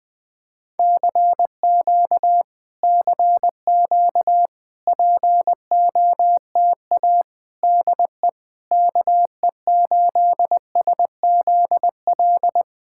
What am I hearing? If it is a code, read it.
Morse code, heard as a keyed tone, about 20 wpm, 700 Hz. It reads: CQ CQ POTA DE KE8SZL
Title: POTA QSO Practice - 20wpm